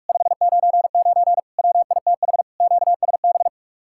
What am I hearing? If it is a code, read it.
Morse code, heard as a keyed tone, about 54 wpm, 700 Hz. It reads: 599 WITH QSB
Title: Speed-Racing Common QSO Phrases - 45wpm